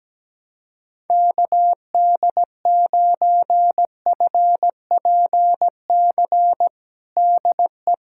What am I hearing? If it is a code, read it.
Morse code, heard as a keyed tone, about 17 wpm, 700 Hz. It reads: KD9FPC DE
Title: USA Callsign DE Callsign - 6 Characters - 17wpm